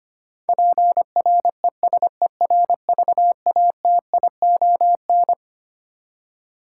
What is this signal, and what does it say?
Morse code, heard as a keyed tone, about 25 wpm, 700 Hz. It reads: PRESERVATION
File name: Twelve Letter Words - 25wpm